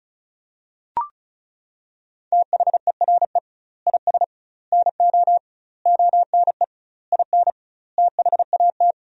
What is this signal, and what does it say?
Morse code, heard as a keyed tone, about 36 wpm, 700 Hz. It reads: THERE IS NO ONE IN THAT
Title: 2x Word-Spacing - Sentences from Top 500 Words - 35wpm